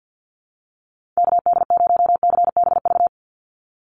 Morse code, heard as a keyed tone, about 48 wpm, 700 Hz. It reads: KD0CBV
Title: USA Call Signs 50wpm